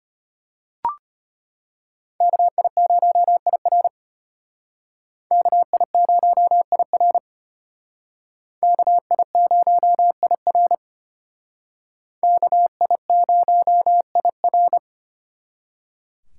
Morse code, heard as a keyed tone, about 32 wpm, 700 Hz. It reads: KI0IR KI0IR KI0IR KI0IR